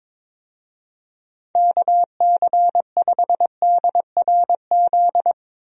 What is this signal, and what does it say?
Morse code, heard as a keyed tone, about 22 wpm, 700 Hz. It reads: KC5DRZ
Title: USA Call Signs - 6 Characters - Repeated 2x - 22wpm